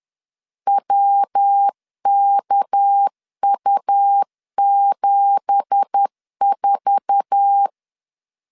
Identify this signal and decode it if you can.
Morse code, heard as a keyed tone, about 10 wpm, 800 Hz. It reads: WKU74
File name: CW / DET / 1 9 1500C S B 10wpm 4439